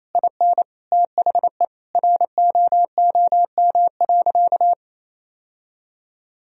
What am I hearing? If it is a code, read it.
Morse code, heard as a keyed tone, about 28 wpm, 700 Hz. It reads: IN THE ROOM.
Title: Sentences from Top 1000 Words - Review Part 4 - 28wpm